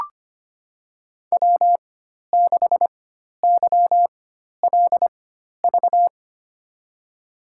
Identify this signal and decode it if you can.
Morse code, heard as a keyed tone, about 25 wpm, spaced as for 12 wpm, 700 Hz. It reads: W6YLV
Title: USA Call Signs - 4x Character Spacing - 25wpm